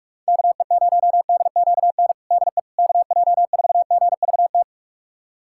Morse code, heard as a keyed tone, about 46 wpm, 700 Hz. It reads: KE0DQN DE KJ4GVT
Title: USA Callsign DE Callsign - 6 Characters - 45wpm